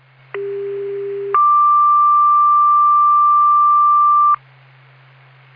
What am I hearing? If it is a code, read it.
Two-tone page, heard as a one-second tone, then a three-second tone, about 400 then 1180 Hz. It is heard from 0.3 to 4.3 s.